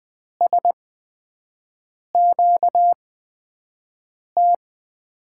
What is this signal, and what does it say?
Morse code, heard as a keyed tone, about 20 wpm, spaced as for 5 wpm, 700 Hz. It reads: SQT
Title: USA Call Signs - 6 Characters - 8x Character Spacing - 20wpm